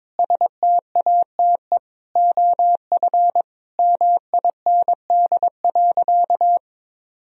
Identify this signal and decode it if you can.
Morse code, heard as a keyed tone, about 22 wpm, 700 Hz. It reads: STATE OF MIND.
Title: Sentences from Top 1000 Words - Review Part 2 - 22wpm